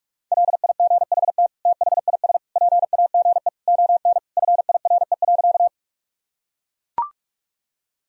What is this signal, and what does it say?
Morse code, heard as a keyed tone, about 41 wpm, 700 Hz. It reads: LIGHT THIS PAGE ON FIRE.
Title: Sentences from Top 1000 Words - Review Part 1 - 45wpm